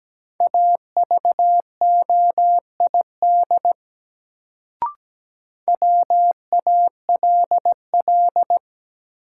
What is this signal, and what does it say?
Morse code, heard as a keyed tone, about 17 wpm, 700 Hz. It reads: AVOID WALL
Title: Sentences from Top 1000 Words - Review Part 6 - New Word Drill - 17wpm